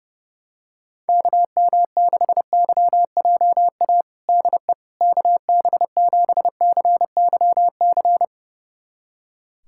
Morse code, heard as a keyed tone, about 30 wpm, 700 Hz. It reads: KM6YJA DE KB7CYC
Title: USA Callsign DE Callsign - 6 Characters - 30wpm